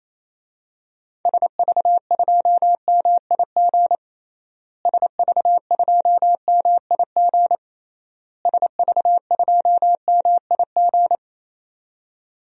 Morse code, heard as a keyed tone, about 27 wpm, 700 Hz. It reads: SV2MIG SV2MIG SV2MIG
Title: DXCC-100 Call Signs - 6 Characters - Repeated 3x - 28wpm